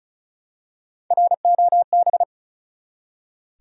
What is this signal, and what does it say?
Morse code, heard as a keyed tone, about 36 wpm, 700 Hz. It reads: ROB
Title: USA Common Names - 35wpm